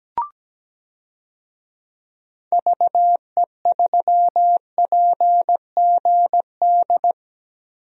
Morse code, heard as a keyed tone, about 17 wpm, 700 Hz. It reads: VE3PGD
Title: DXCC-100 Call Signs - 6 Characters - 17wpm